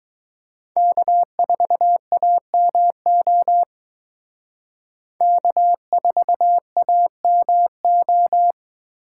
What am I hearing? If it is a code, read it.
Morse code, heard as a keyed tone, about 21 wpm, 700 Hz. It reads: K4AMO K4AMO
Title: Speed Racing - USA 1x3 Call Signs 17wpm